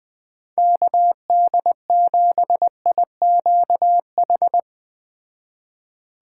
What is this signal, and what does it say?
Morse code, heard as a keyed tone, about 20 wpm, 700 Hz. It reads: KD7IQH